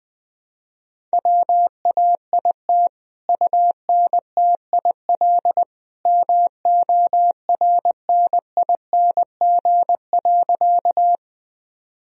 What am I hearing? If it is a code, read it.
Morse code, heard as a keyed tone, about 21 wpm, 700 Hz. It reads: WAIT UNTIL MORNING.
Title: Sentences from Top 1000 Words - Review Part 1 - 20wpm